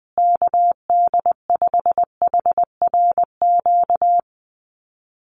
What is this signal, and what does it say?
Morse code, heard as a keyed tone, about 20 wpm, 700 Hz. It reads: KD5HRQ
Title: USA Call Signs Repeated 3x 20wpm